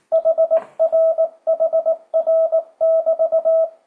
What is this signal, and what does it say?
Morse code, heard as a keyed tone, about 18 wpm, 650 Hz, with radio noise. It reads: HRHR<BT>
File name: 2025cw / speedup / 19wpm